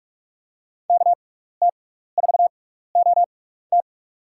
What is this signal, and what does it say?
Morse code, heard as a keyed tone, about 44 wpm, spaced as for 15 wpm, 700 Hz. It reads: KT4OT
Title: USA Call Signs - 5 Characters - 6x Character Spacing - 45wpm